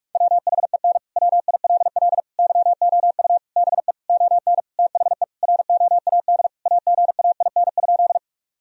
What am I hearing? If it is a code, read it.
Morse code, heard as a keyed tone, about 46 wpm, 700 Hz. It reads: WHEN WILL YOU BE ON THE ROAD AGAIN?
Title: Sentences from Top 1000 Words - Review Part 2 - 45wpm